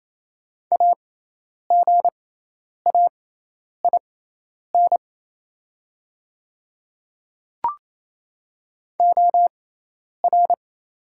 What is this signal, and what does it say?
Morse code, heard as a keyed tone, about 28 wpm, spaced as for 10 wpm, 700 Hz. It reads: AGAIN OR
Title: 6x Character Spacing - Top 300 Words - 28wpm